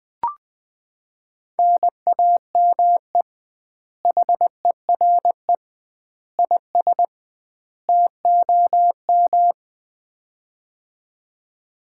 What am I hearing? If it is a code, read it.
Morse code, heard as a keyed tone, about 20 wpm, 700 Hz. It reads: NAME HERE IS TOM
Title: Common QSO Phrases - 2x Word-Spacing - 20wpm